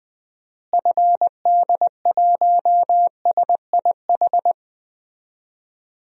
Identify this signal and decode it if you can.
Morse code, heard as a keyed tone, about 20 wpm, 700 Hz. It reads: FD1SIH
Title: DXCC-100 Call Signs - 6 Characters - Repeated 3x - 20wpm